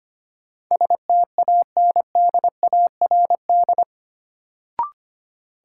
Morse code, heard as a keyed tone, about 25 wpm, 700 Hz. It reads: STANDARD
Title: Sentences from Top 1200 Words - New Word Drill Alt - 25wpm